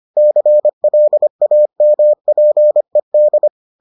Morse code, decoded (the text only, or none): CLAMPED